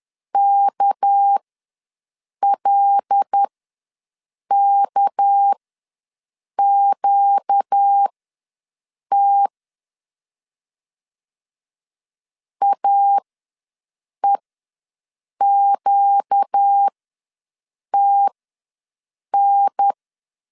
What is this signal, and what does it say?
Morse code, heard as a keyed tone, about 10 wpm, spaced as for 6 wpm, 800 Hz. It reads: KLKQT AEQTN
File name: LEZIONI / 6 PKX0 / 6 PK ws 10wpm 3745